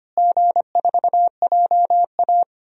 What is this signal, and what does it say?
Morse code, heard as a keyed tone, about 25 wpm, 700 Hz. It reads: G4JA
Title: DXCC-100 Call Signs - 4 Characters - Repeated 2x - 25wpm